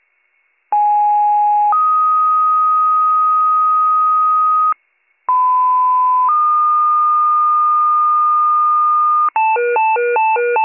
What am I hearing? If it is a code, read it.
Two-tone page, heard as a one-second tone, then a three-second tone, about 830 then 1250 Hz, about 1010 then 1220 Hz. It is heard from 0.7 to 9.3 s.